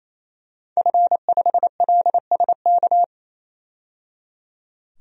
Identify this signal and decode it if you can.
Morse code, heard as a keyed tone, about 28 wpm, 700 Hz. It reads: F5LSK
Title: DXCC-100 Call Signs - 5 Characters - 28wpm